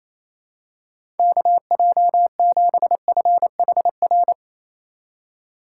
Morse code, heard as a keyed tone, about 28 wpm, 700 Hz. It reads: KJ7FHR
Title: USA Call Signs - 6 Characters - 2x Character Spacing - 28wpm